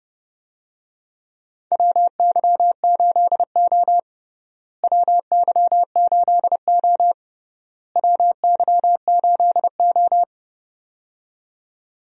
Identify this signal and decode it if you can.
Morse code, heard as a keyed tone, about 30 wpm, 700 Hz. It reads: WY8O WY8O WY8O
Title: USA Call Signs - 4 Characters - Repeated 3x - 30wpm